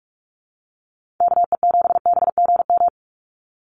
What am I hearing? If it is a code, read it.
Morse code, heard as a keyed tone, about 44 wpm, 700 Hz. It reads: KE7BGM
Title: NA SOTA Activator Callsigns 45wpm